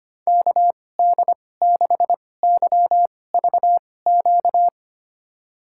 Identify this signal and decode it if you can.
Morse code, heard as a keyed tone, about 25 wpm, spaced as for 18 wpm, 700 Hz. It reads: KD6YVQ
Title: USA Call Signs - 2x Character Spacing - 25wpm